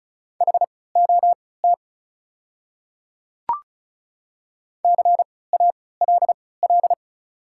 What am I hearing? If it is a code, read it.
Morse code, heard as a keyed tone, about 36 wpm, spaced as for 20 wpm, 700 Hz. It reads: HOT CALL
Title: 3x Character Spacing - Top 100 Words - 35wpm